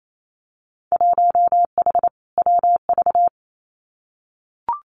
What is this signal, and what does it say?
Morse code, heard as a keyed tone, about 28 wpm, 700 Hz. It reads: 1H WV
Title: Winter Field Day Exchanges Encore 28wpm